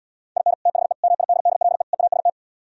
Morse code, heard as a keyed tone, about 77 wpm, 700 Hz. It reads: IN THE MIDDLE EAST